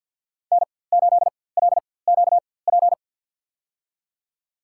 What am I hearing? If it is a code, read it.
Morse code, heard as a keyed tone, about 49 wpm, spaced as for 23 wpm, 700 Hz. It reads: N8LQP
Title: USA Call Signs - 5 Characters - 4x Character Spacing - 50wpm